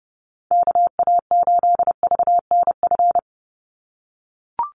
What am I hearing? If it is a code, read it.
Morse code, heard as a keyed tone, about 30 wpm, 700 Hz. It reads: KA8VNF
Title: USA Call Signs 30wpm